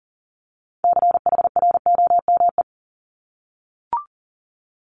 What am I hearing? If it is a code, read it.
Morse code, heard as a keyed tone, about 38 wpm, 690 Hz. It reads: CHROME
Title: Multi-letter Phonemes 40wpm